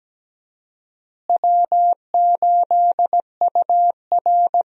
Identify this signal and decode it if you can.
Morse code, heard as a keyed tone, about 17 wpm, 700 Hz. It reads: W8UR
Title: USA Call Signs - 4 Characters - Repeated 2x - 17wpm